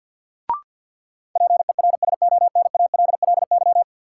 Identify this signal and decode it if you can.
Morse code, heard as a keyed tone, about 46 wpm, 700 Hz. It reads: PERSONALLY
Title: Sentences from Top 1400 Words - New Word Drill - 50wpm